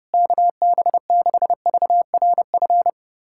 Morse code, heard as a keyed tone, about 30 wpm, 700 Hz. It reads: KB6VRF
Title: USA Callsign DE Callsign - 6 Characters - 30wpm